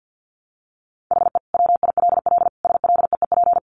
Morse code, heard as a keyed tone, about 50 wpm, 700 Hz. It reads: HE WILL SLEEP
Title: Sentences from Top 500 Words 50wpm